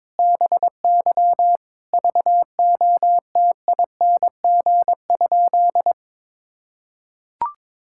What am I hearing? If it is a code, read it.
Morse code, heard as a keyed tone, about 22 wpm, 700 Hz. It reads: BY VOTING?
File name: Sentences with 9-Word Phrases - 22wpm